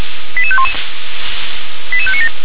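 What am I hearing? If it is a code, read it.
Five-tone selective call (ZVEI), heard as quick runs of five tones, tones of about 70 ms. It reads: 88511 88597
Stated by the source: Received with an SDR; ZVEI five-tone call 88502.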